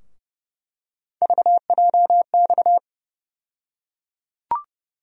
Morse code, heard as a keyed tone, about 30 wpm, 700 Hz. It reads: VJX
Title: USA Call Sign Postfixes - 3 Letters 30wpm